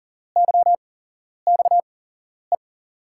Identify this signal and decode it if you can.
Morse code, heard as a keyed tone, about 40 wpm, spaced as for 11 wpm, 700 Hz. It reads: YXE
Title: USA Call Signs - 6 Characters - 8x Character Spacing - 40wpm